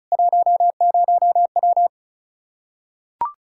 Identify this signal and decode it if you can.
Morse code, heard as a keyed tone, about 34 wpm, 700 Hz. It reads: 10W
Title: Practice QSO - Exchange 2 - Info - 35wpm